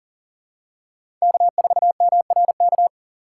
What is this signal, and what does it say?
Morse code, heard as a keyed tone, about 40 wpm, 700 Hz. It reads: K4MRK
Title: USA Call Signs - 5 Characters - 2x Character Spacing - 40wpm